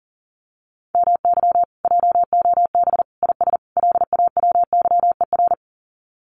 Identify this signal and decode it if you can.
Morse code, heard as a keyed tone, about 40 wpm, 700 Hz. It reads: MY JOB IS LAWYER
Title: Common QSO Phrases Encore 40wpm